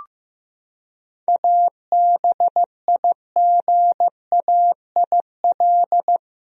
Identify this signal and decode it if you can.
Morse code, heard as a keyed tone, about 15 wpm, 700 Hz. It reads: ABIGAIL
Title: USA Common Names - 15wpm